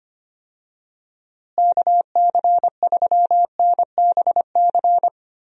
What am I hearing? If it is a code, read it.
Morse code, heard as a keyed tone, about 25 wpm, 700 Hz. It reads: KC3NBC
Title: USA Call Signs - 6 Characters - Repeated 2x - 25wpm